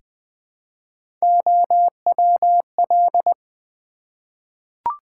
Morse code, heard as a keyed tone, about 20 wpm, 700 Hz. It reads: OWL
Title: Easy Words with Hints - 20wpm